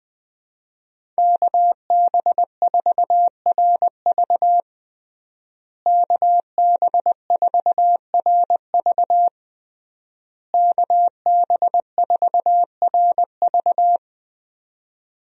Morse code, heard as a keyed tone, about 21 wpm, 700 Hz. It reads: KB4RV KB4RV KB4RV
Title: USA Call Signs - 5 Characters - Repeated 3x - 20wpm